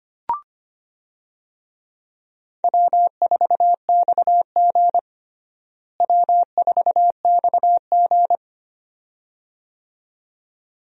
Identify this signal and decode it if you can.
Morse code, heard as a keyed tone, about 25 wpm, 700 Hz. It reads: W4XG W4XG
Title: USA Call Signs - 4 Characters - Repeated 2x - 25wpm